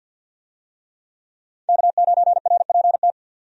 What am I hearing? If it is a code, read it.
Morse code, heard as a keyed tone, about 52 wpm, 700 Hz. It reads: K9RPT
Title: USA Call Signs - 5 Characters - 50wpm